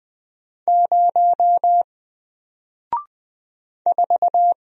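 Morse code, heard as a keyed tone, about 20 wpm, 700 Hz. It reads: 0 4
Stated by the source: Single letter and number Morse drill at 20 wpm, from N4LCD.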